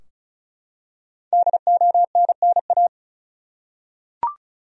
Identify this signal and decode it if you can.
Morse code, heard as a keyed tone, about 34 wpm, 700 Hz. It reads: DONNA
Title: USA Common Names - Max 5 Letters - 35wpm